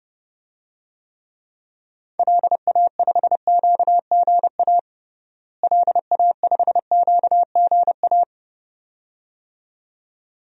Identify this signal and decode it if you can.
Morse code, heard as a keyed tone, about 30 wpm, 700 Hz. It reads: LA5QGA LA5QGA
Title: DXCC-100 Call Signs - 6 Characters - Repeated 2x - 30wpm